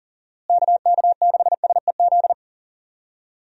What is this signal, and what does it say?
Morse code, heard as a keyed tone, about 40 wpm, 700 Hz. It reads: KK6SEZ
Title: USA Call Signs - 6 Characters - Repeated 2x - 40wpm